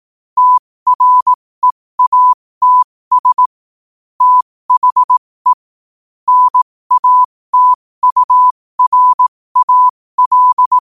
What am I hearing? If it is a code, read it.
Morse code, heard as a keyed tone, about 18 wpm, spaced as for 15 wpm, 1000 Hz. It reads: TREATS THE NATURAL
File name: qotdfiles / 15wpm